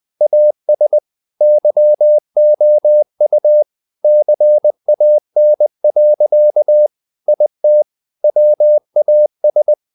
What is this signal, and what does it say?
Morse code, heard as a keyed tone, about 20 wpm, 600 Hz. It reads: AS YOU CAN. IT WAS
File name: the old man and the sea / the old man and the sea part 1 20wpm